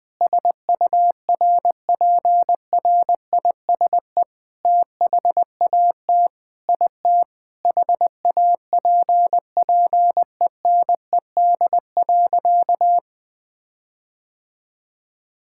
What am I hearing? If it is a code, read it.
Morse code, heard as a keyed tone, about 20 wpm, 700 Hz. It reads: SURPRISE THAT IT HAPPENED.
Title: Sentences with 8-Word Phrases - 20wpm